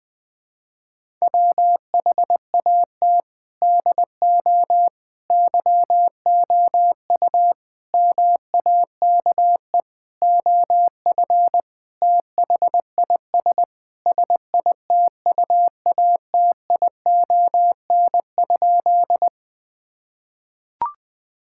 Morse code, heard as a keyed tone, about 19 wpm, 700 Hz. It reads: WHAT DO YOU MAKE OF THIS SITUATION?
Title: Sentences with 5-Word Phrases - Spoken Beginning - 20wpm